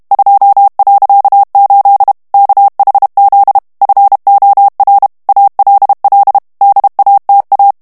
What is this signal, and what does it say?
Morse code, heard as a keyed tone, about 32 wpm, 800 Hz. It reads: 2.8 KHZ FOR ALL DATA
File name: ANGLAIS texte clair 3 vitesse 30 mots minute